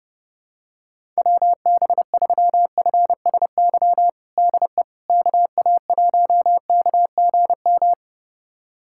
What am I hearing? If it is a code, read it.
Morse code, heard as a keyed tone, about 30 wpm, 700 Hz. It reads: WB3FSY DE KA1KGM
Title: USA Callsign DE Callsign - 6 Characters - 30wpm